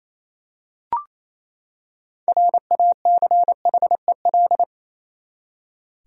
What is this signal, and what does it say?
Morse code, heard as a keyed tone, about 28 wpm, 700 Hz. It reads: RACHEL
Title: USA Common Names - 28wpm